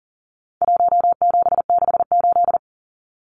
Morse code, heard as a keyed tone, about 40 wpm, 700 Hz. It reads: 1768
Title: Numbers - Contest Style - 4-Digit 40wpm